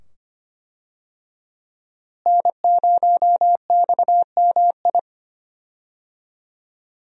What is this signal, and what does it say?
Morse code, heard as a keyed tone, about 25 wpm, 700 Hz. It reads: N0XMI
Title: USA Call Signs - 5 Characters - Repeated 2x - 25wpm